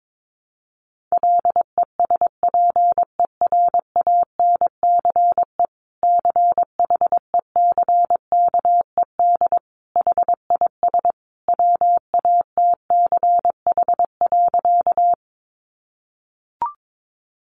Morse code, heard as a keyed tone, about 22 wpm, 700 Hz. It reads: LESPERANCE CHECKED HIS WATCH.
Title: A Sound Of Thunder 22wpm